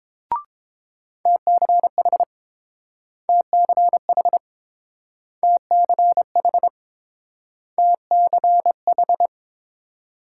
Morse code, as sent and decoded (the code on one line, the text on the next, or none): - -.-. .... / - -.-. .... / - -.-. .... / - -.-. ....
TCH TCH TCH TCH